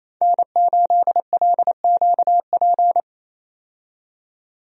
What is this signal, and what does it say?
Morse code, heard as a keyed tone, about 28 wpm, 700 Hz. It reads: N8LQP
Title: USA Call Signs - 5 Characters - 28wpm